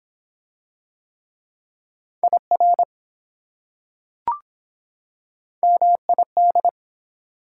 Morse code, heard as a keyed tone, about 23 wpm, 700 Hz. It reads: IR MID